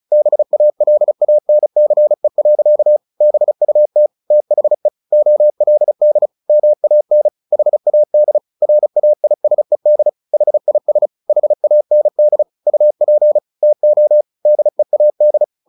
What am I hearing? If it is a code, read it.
Morse code, heard as a keyed tone, about 34 wpm, 600 Hz. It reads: BALANCE. BUT THE OLD MAN HAD RAISED HIS HAND UP TO DEAD
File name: the old man and the sea 2 35wpm